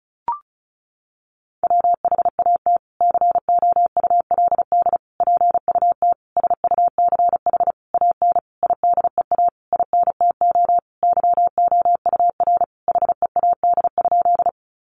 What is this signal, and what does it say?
Morse code, heard as a keyed tone, about 34 wpm, 700 Hz. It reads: WHAT COULD PUT SUCH AN IDEA INTO YOUR HEAD?
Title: Sentences from Top 600 Words 35wpm